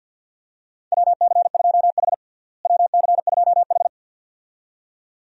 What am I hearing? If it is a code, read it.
Morse code, heard as a keyed tone, about 52 wpm, 700 Hz. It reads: WC2H WC2H
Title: USA Call Signs - 4 Characters - Repeated 2x - 50wpm